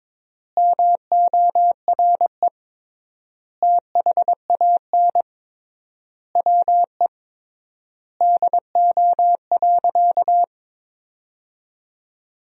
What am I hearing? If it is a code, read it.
Morse code, heard as a keyed tone, about 22 wpm, 700 Hz. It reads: MORE THAN WE DO.
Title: 3x Word-Spacing - Sentences from Top 100 Words - 22wpm